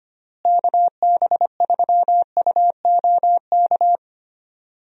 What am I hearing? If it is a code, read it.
Morse code, heard as a keyed tone, about 25 wpm, 700 Hz. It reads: KB3UOK
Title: USA Call Signs - 6 Characters - Repeated 3x - 25wpm